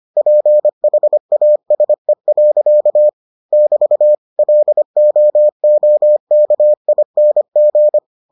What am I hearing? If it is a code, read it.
Morse code, heard as a keyed tone, about 25 wpm, 600 Hz. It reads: PHASE. <BT> LOOKING